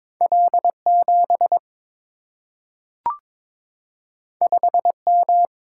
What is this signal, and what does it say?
Morse code, heard as a keyed tone, about 22 wpm, 700 Hz. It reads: L 7 5 M
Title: Single-Letter Single-Number 22wpm